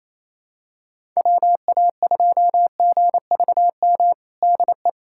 Morse code, heard as a keyed tone, about 28 wpm, 700 Hz. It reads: WA2GVM DE
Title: USA Callsign DE Callsign - 6 Characters - 28wpm